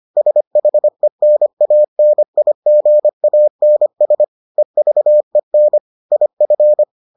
Morse code, heard as a keyed tone, about 25 wpm, 600 Hz. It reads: SHENANIGANS EVEN IF